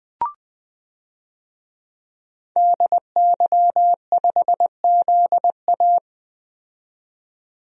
Morse code, heard as a keyed tone, about 20 wpm, 700 Hz. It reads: DY5ZA
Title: DXCC-100 Call Signs - 5 Characters - 20wpm